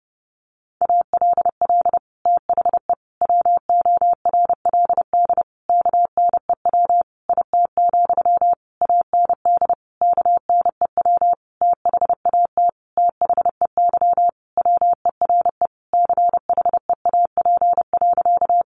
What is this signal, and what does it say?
Morse code, heard as a keyed tone, about 28 wpm, 700 Hz. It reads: ALL THE WORLD KNEW IT, AND KNEW THAT THEY WERE CHEAP.
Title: Sentences from Top 1200 Words 30wpm